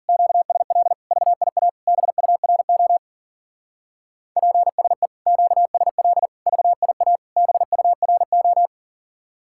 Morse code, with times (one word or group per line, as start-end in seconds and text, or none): 0.1-0.9 QSL
1.1-1.7 VIA
1.9-3.0 BURO
4.4-5.1 PSE
5.3-6.3 QSL
6.5-7.2 VIA
7.4-8.7 BURO